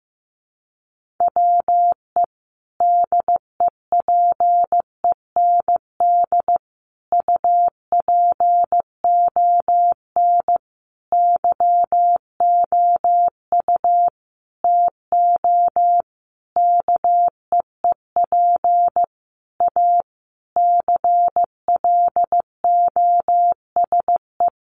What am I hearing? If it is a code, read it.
Morse code, heard as a keyed tone, about 15 wpm, 700 Hz. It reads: WE DEPEND UPON YOU TO KEEP A CLOSE
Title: Sentences from Top 1300 Words 15wpm